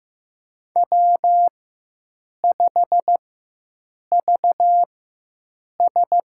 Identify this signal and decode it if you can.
Morse code, heard as a keyed tone, about 15 wpm, spaced as for 7 wpm, 700 Hz. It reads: W5VS
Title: USA Call Signs - 4 Characters - 4x Character Spacing - 15wpm